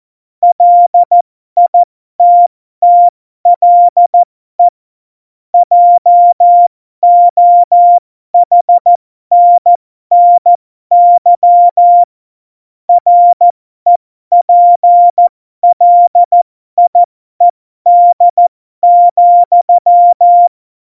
Morse code, heard as a keyed tone, about 14 wpm, spaced as for 12 wpm, 700 Hz. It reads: LITTLE JOHNNY REPLIED,
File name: Morsekurs / Witze